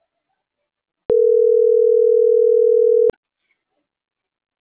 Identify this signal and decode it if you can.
Ringback tone (North America), 440 + 480 Hz.